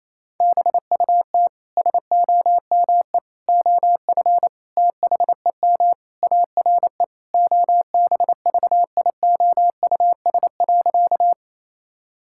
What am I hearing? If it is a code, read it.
Morse code, heard as a keyed tone, about 28 wpm, 700 Hz. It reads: BUT SOME OF THEM ARE OBVIOUS.
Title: Sentences from Top 1600 Words 28wpm